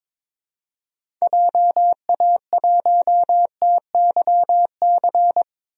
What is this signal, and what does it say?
Morse code, heard as a keyed tone, about 22 wpm, 700 Hz. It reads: JA1TYC
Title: DXCC-100 Call Signs - 6 Characters - Repeated 2x - 22wpm